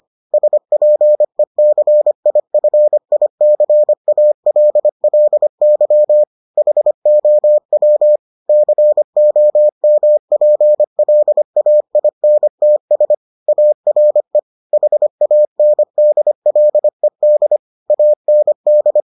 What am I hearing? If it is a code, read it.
Morse code, heard as a keyed tone, about 25 wpm, 600 Hz. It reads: SPECIFICALLY HOW COMPLAINTS ARE HANDLED AND